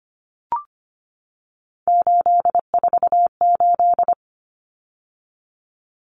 Morse code, heard as a keyed tone, about 25 wpm, 700 Hz. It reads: 848
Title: Common Cut Numbers - 3-Digit 25wpm